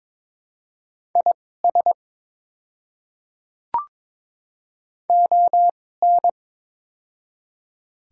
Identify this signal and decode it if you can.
Morse code, heard as a keyed tone, about 22 wpm, spaced as for 16 wpm, 690 Hz. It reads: IS ON
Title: 2x Character Spacing - Top 100 Words - 22wpm